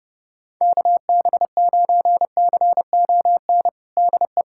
Morse code, heard as a keyed tone, about 30 wpm, 700 Hz. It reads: KB9CON DE
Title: USA Callsign DE Callsign 30wpm